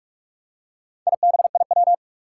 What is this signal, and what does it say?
Morse code, heard as a keyed tone, about 44 wpm, 700 Hz. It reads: IBIW
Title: Four Challenging Letter-Number Combinations 45wpm